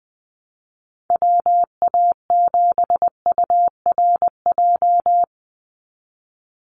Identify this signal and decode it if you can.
Morse code, heard as a keyed tone, about 20 wpm, 700 Hz. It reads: WA7URJ
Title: USA Call Signs 20wpm